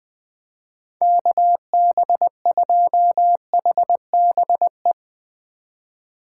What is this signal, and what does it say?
Morse code, heard as a keyed tone, about 21 wpm, 700 Hz. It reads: KB2HBE
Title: USA Call Signs - 6 Characters - 2x Character Spacing - 20wpm